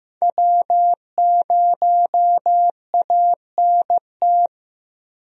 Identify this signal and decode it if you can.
Morse code, heard as a keyed tone, about 15 wpm, 700 Hz. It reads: W0ANT
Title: USA Call Signs - 5 Characters - Repeated 2x - 15wpm